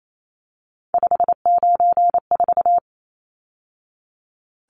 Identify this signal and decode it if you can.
Morse code, heard as a keyed tone, about 28 wpm, 700 Hz. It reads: 594
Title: Numbers - 3-Digit 28wpm